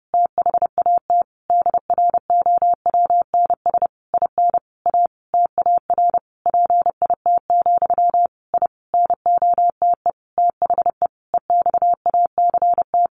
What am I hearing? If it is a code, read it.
Morse code, heard as a keyed tone, about 30 wpm, 700 Hz. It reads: THAT DROWNS IN A TAR PIT, I NOTE THE EXACT
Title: A Sound Of Thunder 30wpm